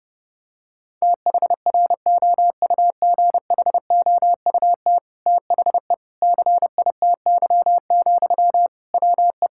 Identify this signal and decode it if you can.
Morse code, heard as a keyed tone, about 30 wpm, 700 Hz. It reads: THROUGHOUT THE CITY, WE
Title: Sentences from Top 1000 Words - Review Part 5 - 30wpm